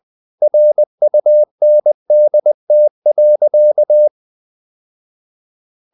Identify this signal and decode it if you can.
Morse code, heard as a keyed tone, about 20 wpm, 600 Hz. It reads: RUNDT.
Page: Ferie 20 wpm